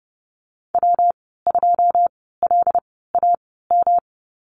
Morse code, heard as a keyed tone, about 30 wpm, spaced as for 17 wpm, 700 Hz. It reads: W2LAM
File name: Lesson 19g - Letter w - Call Signs 30wpm